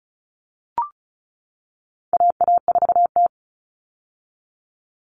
Morse code, heard as a keyed tone, about 34 wpm, 700 Hz. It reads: AA4T
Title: All Cut Numbers - 4-Digit 35wpm